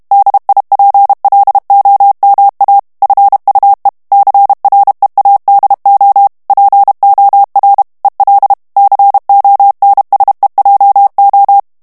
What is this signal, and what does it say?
Morse code, heard as a keyed tone, about 32 wpm, 800 Hz. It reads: DIPLOMA FUE CREADO POR EL CONSEJO
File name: morse mp3 / ESPAGNOL TEXTE EN CLAIR 1 VITESSE 30 MOTS MINUTES